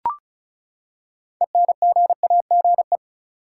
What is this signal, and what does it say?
Morse code, heard as a keyed tone, about 36 wpm, 700 Hz. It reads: ENGAGE
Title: Six Letter Words - 35wpm